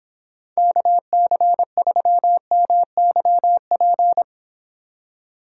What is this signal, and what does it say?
Morse code, heard as a keyed tone, about 26 wpm, 700 Hz. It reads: KC3MYP